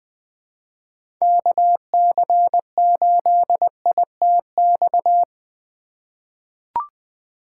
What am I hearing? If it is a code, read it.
Morse code, heard as a keyed tone, about 20 wpm, 700 Hz. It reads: KC8ITX
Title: USA Call Signs - 6 Characters - 2x Character Spacing - 20wpm